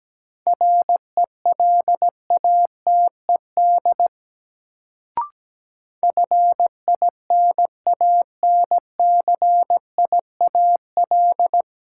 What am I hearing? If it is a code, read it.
Morse code, heard as a keyed tone, about 17 wpm, 700 Hz. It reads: RELATED FINANCIAL
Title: Sentences from Top 1000 Words - Review Part 5 - New Word Drill - 17wpm